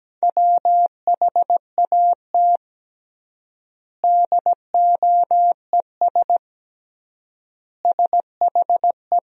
Morse code, heard as a keyed tone, about 17 wpm, 700 Hz. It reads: WHAT DOES SHE
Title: 3x Word-Spacing - Sentences from Top 200 Words - 17wpm